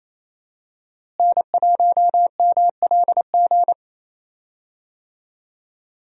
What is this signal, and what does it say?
Morse code, heard as a keyed tone, about 28 wpm, 700 Hz. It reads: N1MLG
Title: USA Call Signs - 5 Characters - Repeated 2x - 28wpm